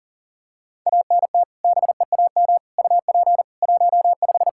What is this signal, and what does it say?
Morse code, heard as a keyed tone, about 40 wpm, 700 Hz. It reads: ANT BEAM UP 15
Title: Practice QSO - Exchange 2 - Info - 40wpm